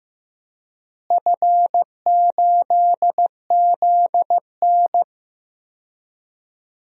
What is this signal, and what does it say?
Morse code, heard as a keyed tone, about 15 wpm, 700 Hz. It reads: F8ZN
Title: DXCC-100 Call Signs - 4 Characters - Repeated 3x - 15wpm